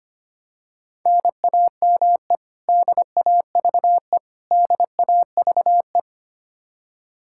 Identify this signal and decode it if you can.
Morse code, heard as a keyed tone, about 25 wpm, 700 Hz. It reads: NAME DAVE DAVE
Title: Practice QSO - Exchange 1 - Beginning - 25wpm